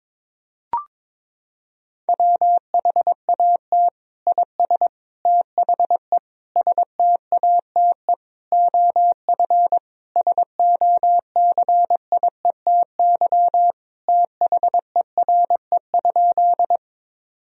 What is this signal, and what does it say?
Morse code, heard as a keyed tone, about 22 wpm, 700 Hz. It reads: WHAT IS THE STATE OF SOCIETY THERE?
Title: Sentences from Top 1000 Words - Review Part 4 - 22wpm